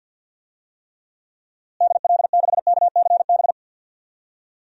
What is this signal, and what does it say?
Morse code, heard as a keyed tone, about 50 wpm, 700 Hz. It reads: DL6KCB
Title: DXCC-100 Call Signs - 6 Characters - Repeated 3x - 50wpm